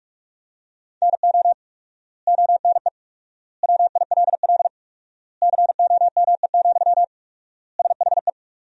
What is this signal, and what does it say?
Morse code, heard as a keyed tone, about 46 wpm, 700 Hz. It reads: NO ONE WILL COME, SHE
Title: 4x Word-Spacing - Sentences from Top 100 Words - 45wpm